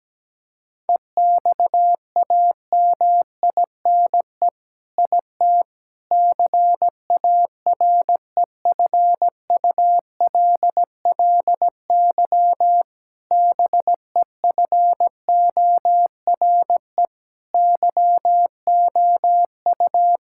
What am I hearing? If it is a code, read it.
Morse code, heard as a keyed tone, about 17 wpm, 700 Hz. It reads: EXAMINE IT CAREFULLY BEFORE YOU
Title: Sentences from Top 1600 Words 17wpm